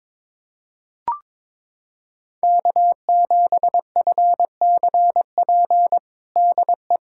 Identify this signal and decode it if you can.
Morse code, heard as a keyed tone, about 22 wpm, 700 Hz. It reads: K7FCP DE
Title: USA Callsign DE Callsign - 5 Characters - 22wpm